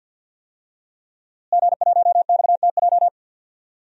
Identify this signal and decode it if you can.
Morse code, heard as a keyed tone, about 52 wpm, 700 Hz. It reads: G1XTJ
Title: DXCC-100 Call Signs - 5 Characters - Repeated 3x - 50wpm